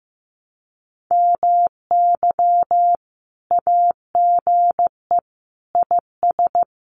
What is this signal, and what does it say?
Morse code, heard as a keyed tone, about 15 wpm, 700 Hz. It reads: MY AGE IS
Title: Common QSO Phrases Encore 15wpm